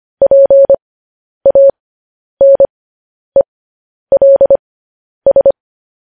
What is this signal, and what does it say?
Morse code, heard as a keyed tone, about 25 wpm, spaced as for 10 wpm, 550 Hz. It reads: PANELS
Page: News Headlines in Morse Code at 10 WPM